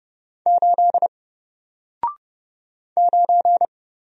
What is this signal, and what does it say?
Morse code, heard as a keyed tone, about 30 wpm, 700 Hz. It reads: 8 9
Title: Single Letter-Number Alt 30wpm